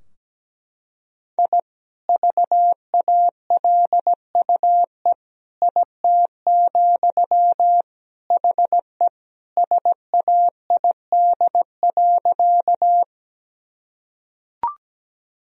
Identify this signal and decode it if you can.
Morse code, heard as a keyed tone, about 17 wpm, 700 Hz. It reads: I VALUE IT, HE SAID.
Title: Sentences from Top 1000 Words - Review Part 3 - 17wpm